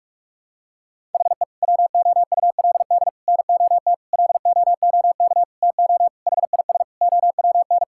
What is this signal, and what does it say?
Morse code, heard as a keyed tone, about 46 wpm, 700 Hz. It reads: HE WOULD NOT LOOK TO HIS OWN
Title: Sentences from Top 1000 Words - Review Part 6 - 45wpm